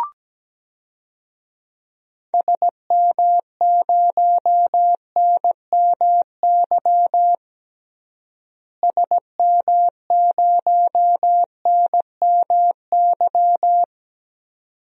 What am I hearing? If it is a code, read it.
Morse code, heard as a keyed tone, about 17 wpm, 700 Hz. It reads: SM0NMY SM0NMY
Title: DXCC-100 Call Signs - 6 Characters - Repeated 2x - 17wpm